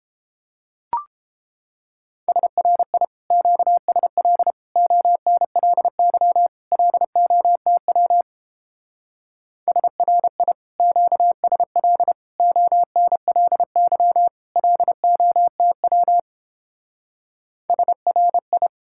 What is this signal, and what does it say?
Morse code, heard as a keyed tone, about 29 wpm, 700 Hz. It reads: SRI QSL ONLY LOTW SRI QSL ONLY LOTW SRI